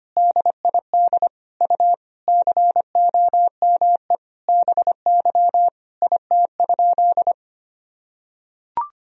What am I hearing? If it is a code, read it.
Morse code, heard as a keyed tone, about 25 wpm, 700 Hz. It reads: DID U COME BY IT?
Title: Sentences from Top 100 Words and Abbreviations 25wpm